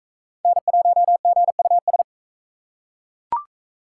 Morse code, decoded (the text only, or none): N1GUS